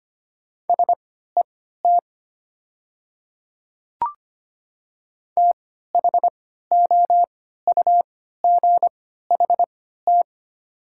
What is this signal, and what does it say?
Morse code, heard as a keyed tone, about 25 wpm, spaced as for 14 wpm, 700 Hz. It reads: SET THOUGHT
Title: 3x Character Spacing - Top 300 Words - 25wpm